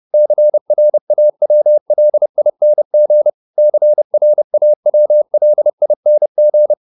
Morse code, heard as a keyed tone, about 30 wpm, 600 Hz. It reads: CRAWLING CRAWLING
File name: ING8 30